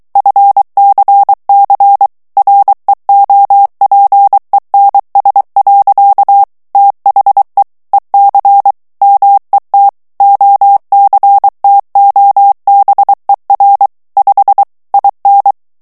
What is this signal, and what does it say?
Morse code, heard as a keyed tone, about 23 wpm, 800 Hz. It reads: FCC REOPENS. THE EC MET OCTOBER 5 IN
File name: ANGLAIS texte clair 3 vitesse 22 mots minute